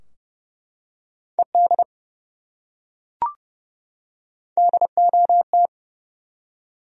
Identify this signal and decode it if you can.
Morse code, heard as a keyed tone, about 29 wpm, 700 Hz. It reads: ED DOT
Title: USA Common Names - Max 3 Letters - 30wpm